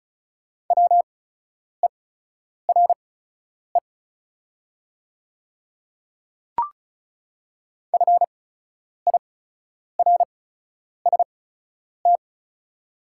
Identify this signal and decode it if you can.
Morse code, heard as a keyed tone, about 36 wpm, spaced as for 10 wpm, 700 Hz. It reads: WERE FIRST
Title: 8x Character Spacing - Top 100 Words - 35wpm